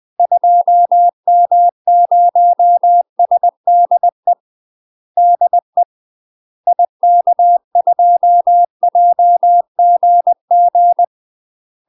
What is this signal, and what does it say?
Morse code, heard as a keyed tone, about 20 wpm, 700 Hz. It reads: 2M0SDE DE IK2JGG
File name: UK MASTER QSOs TRAINING / 20WPM / UK MASTER QSOs 0063